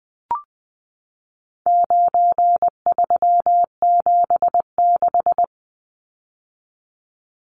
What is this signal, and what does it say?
Morse code, heard as a keyed tone, about 20 wpm, 700 Hz. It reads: 9376
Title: Numbers - 4-Digit 20wpm